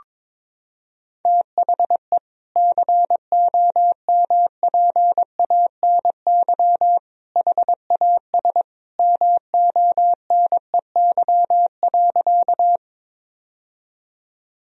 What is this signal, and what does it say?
Morse code, heard as a keyed tone, about 22 wpm, 700 Hz. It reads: THE COMPANY HAS MONEY.
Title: Sentences from Top 1000 Words - Review Part 3 - 22wpm